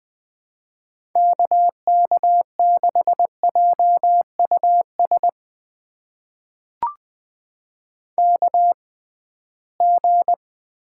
Morse code, heard as a keyed tone, about 20 wpm, spaced as for 15 wpm, 700 Hz. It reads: KK6JUS KG
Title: USA Call Signs - 6 Characters - 6x Character Spacing - 20wpm